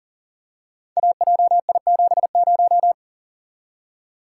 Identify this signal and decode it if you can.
Morse code, heard as a keyed tone, about 40 wpm, 700 Hz. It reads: AJI70
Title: Code Groups - 5 Letter:Numbers - 40wpm